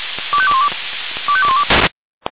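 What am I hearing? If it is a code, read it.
Five-tone selective call (ZVEI), heard as quick runs of five tones, tones of about 70 ms. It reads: 25212 25212
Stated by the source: ZVEI five-tone selective call received on an SDR.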